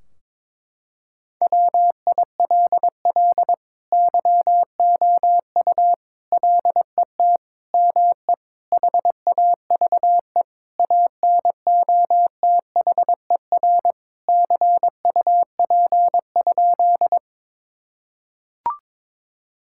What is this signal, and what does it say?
Morse code, heard as a keyed tone, about 22 wpm, 700 Hz. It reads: WILL YOU LET ME HAVE ANOTHER CUP?
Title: Sentences from Top 1600 Words 22wpm